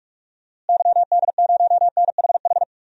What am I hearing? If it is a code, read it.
Morse code, heard as a keyed tone, about 46 wpm, 700 Hz. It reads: YD0NHH
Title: DXCC-100 Call Signs - 6 Characters - Repeated 2x - 45wpm